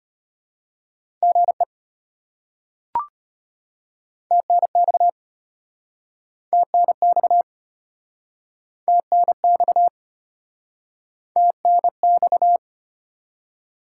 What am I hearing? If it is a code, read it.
Morse code, heard as a keyed tone, about 32 wpm, 700 Hz. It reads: GE TNX TNX TNX TNX